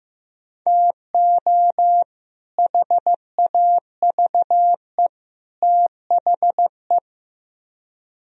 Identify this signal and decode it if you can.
Morse code, heard as a keyed tone, about 16 wpm, 700 Hz. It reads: TO HAVE THE
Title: Speed-Racing - Sets of 3 Words 15wpm